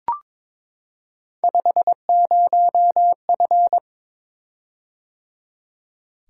Morse code, heard as a keyed tone, about 22 wpm, 700 Hz. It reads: 50F
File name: practice / 22wpm / Three Letter-Number Combinations - 22wpm